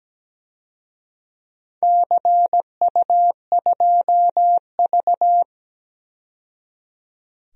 Morse code, heard as a keyed tone, about 17 wpm, 700 Hz. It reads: CU2V
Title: DXCC-100 Call Signs - 4 Characters - 17wpm